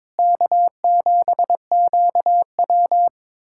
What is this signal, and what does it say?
Morse code, heard as a keyed tone, about 22 wpm, 700 Hz. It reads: K7QW
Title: USA Callsign DE Callsign - 4 Characters - 22wpm